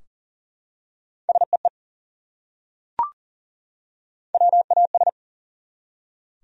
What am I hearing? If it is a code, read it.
Morse code, heard as a keyed tone, about 40 wpm, 700 Hz. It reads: SEE WAS
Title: Top 100 Words 40wpm